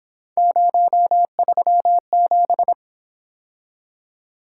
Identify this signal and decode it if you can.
Morse code, heard as a keyed tone, about 26 wpm, 700 Hz. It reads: 037
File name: Speed-Racing - Three Letter-Number Combinations - 22wpm